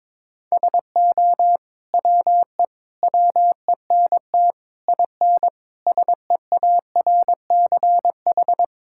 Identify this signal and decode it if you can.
Morse code, heard as a keyed tone, about 22 wpm, 700 Hz. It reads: SO WE WENT IN SEARCH
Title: Sentences from Top 1000 Words - Review Part 5 - 22wpm